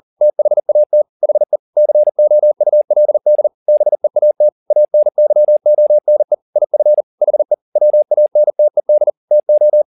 Morse code, heard as a keyed tone, about 40 wpm, 600 Hz. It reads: THAT HE COULD BEAT ANYONE IF HE WANTED TO
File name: the old man and the sea 2 40wpm